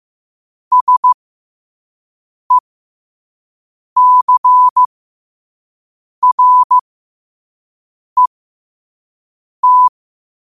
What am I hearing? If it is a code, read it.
Morse code, heard as a keyed tone, about 15 wpm, spaced as for 5 wpm, 1000 Hz. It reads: SECRET
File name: 5wpm